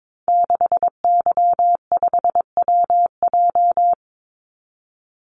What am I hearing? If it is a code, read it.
Morse code, heard as a keyed tone, about 22 wpm, 700 Hz. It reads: 6Y5WJ
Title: Call Signs Repeated 3x - 22wpm